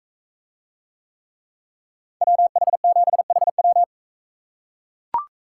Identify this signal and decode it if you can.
Morse code, heard as a keyed tone, about 40 wpm, 700 Hz. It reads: WH7HW